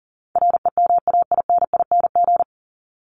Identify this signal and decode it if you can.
Morse code, heard as a keyed tone, about 40 wpm, 700 Hz. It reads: REMAINING
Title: ING Suffix - Top 100 Words with Base Words 40wpm